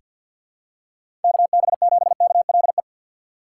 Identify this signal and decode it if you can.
Morse code, heard as a keyed tone, about 49 wpm, 700 Hz. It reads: KB7KLE
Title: USA Call Signs - 6 Characters - 4x Character Spacing - 50wpm